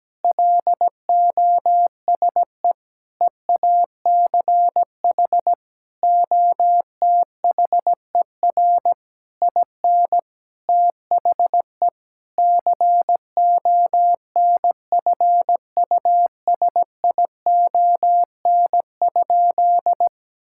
Morse code, heard as a keyed tone, about 17 wpm, 700 Hz. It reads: LOSE EACH OTHER IN THE CONFUSION?
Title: Sentences from Top 1600 Words 17wpm